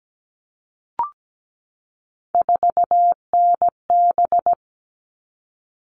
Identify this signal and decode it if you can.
Morse code, heard as a keyed tone, about 17 wpm, 700 Hz. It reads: 4NB
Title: All Cut Numbers - 3-Digit 17wpm